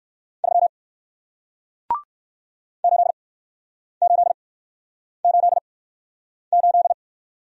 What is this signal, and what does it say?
Morse code, heard as a keyed tone, about 61 wpm, spaced as for 9 wpm, 700 Hz. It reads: 3 8888